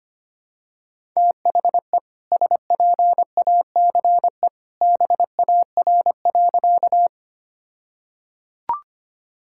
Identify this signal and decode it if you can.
Morse code, heard as a keyed tone, about 25 wpm, 700 Hz. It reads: THE SPACE BAR.
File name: practice / 25wpm / All-Age Appropriate Jokes - 25wpm